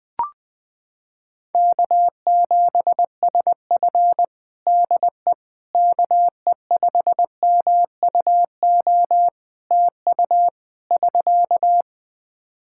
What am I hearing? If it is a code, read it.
Morse code, heard as a keyed tone, about 21 wpm, 700 Hz. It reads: K7SF DE KE5MUO TU <SK>
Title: K1USN SST QSO Practice 20wpm